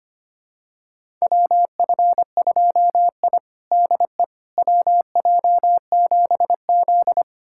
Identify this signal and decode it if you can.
Morse code, heard as a keyed tone, about 25 wpm, 700 Hz. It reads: WF2I DE WJ7Z
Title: USA Callsign DE Callsign - 4 Characters - 25wpm